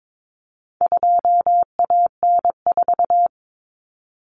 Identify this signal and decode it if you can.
Morse code, heard as a keyed tone, about 22 wpm, 700 Hz. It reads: 2AN4
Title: Common Cut Numbers - 4-Digit 22wpm